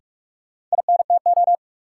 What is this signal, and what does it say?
Morse code, heard as a keyed tone, about 44 wpm, 700 Hz. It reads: INTO
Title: Sets of 2 Words - New Word Drill - 45wpm